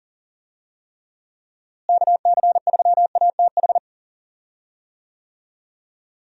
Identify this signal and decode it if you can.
Morse code, heard as a keyed tone, about 40 wpm, 700 Hz. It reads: KC3ATH
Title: USA Call Signs - 6 Characters - Repeated 2x - 40wpm